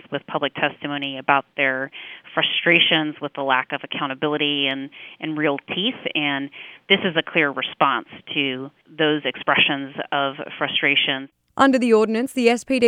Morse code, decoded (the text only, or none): none